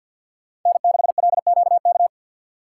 Morse code, heard as a keyed tone, about 51 wpm, 700 Hz. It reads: N6LQK
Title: USA Call Signs - 5 Characters - 3x Character Spacing - 50wpm